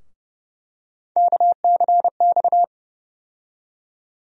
Morse code, heard as a keyed tone, about 32 wpm, 700 Hz. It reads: KCX
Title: USA Call Sign Postfixes - 3 Letters 30wpm